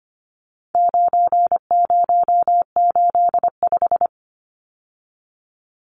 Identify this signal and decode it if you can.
Morse code, heard as a keyed tone, about 25 wpm, 700 Hz. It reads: 9085
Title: Numbers - 4-Digit 25wpm